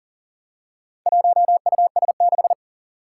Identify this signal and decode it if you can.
Morse code, heard as a keyed tone, about 42 wpm, 700 Hz. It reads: 1US6
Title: Four Challenging Letter-Number Combinations 40wpm